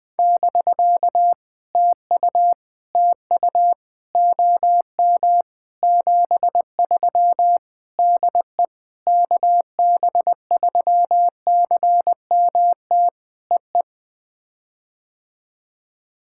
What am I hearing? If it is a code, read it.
Morse code, heard as a keyed tone, about 19 wpm, 700 Hz. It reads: <BK> TU TU OM 73 DE KB3CMT EE
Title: POTA QSO Practice - 20wpm